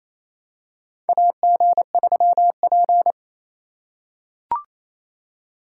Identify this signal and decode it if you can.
Morse code, heard as a keyed tone, about 28 wpm, 700 Hz. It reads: AG3P
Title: USA Call Signs - 4 Characters - 3x Character Spacing - 28wpm